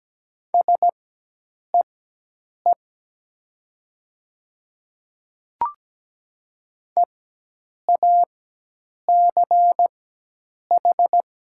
Morse code, heard as a keyed tone, about 17 wpm, spaced as for 8 wpm, 700 Hz. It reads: SEE EACH
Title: 4x Character Spacing - Top 100 Words - 17wpm